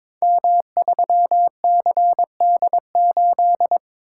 Morse code, heard as a keyed tone, about 22 wpm, 690 Hz. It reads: M3CD8
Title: Code Groups - 5 Letter:Numbers - 22wpm